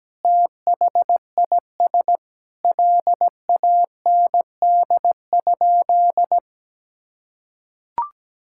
Morse code, heard as a keyed tone, about 17 wpm, 700 Hz. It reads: THIS LAND?
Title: Sentences from Top 1600 Words 17wpm